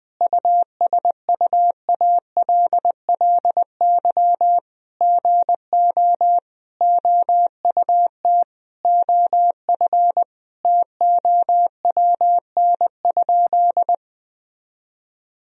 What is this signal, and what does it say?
Morse code, heard as a keyed tone, about 20 wpm, 700 Hz. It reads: USUALLY GO OUT OF TOWN?
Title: Sentences from Top 1000 Words - Review Part 3 - 20wpm